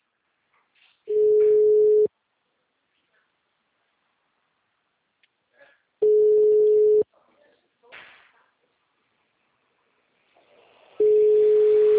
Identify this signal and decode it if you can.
Ringback tone (Europe, 425 Hz), with speech.
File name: out- -2025- - - .51366